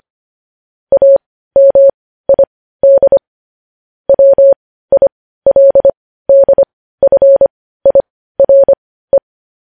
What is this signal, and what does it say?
Morse code, heard as a keyed tone, about 25 wpm, spaced as for 15 wpm, 550 Hz. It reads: AMID WILDFIRE
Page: news headlines in morse code at 15 wpm